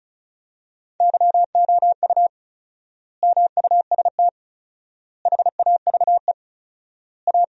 Morse code, heard as a keyed tone, about 34 wpm, 700 Hz. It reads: YOU MUST HAVE A
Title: 4x Word-Spacing - Sentences from Top 500 Words - 35wpm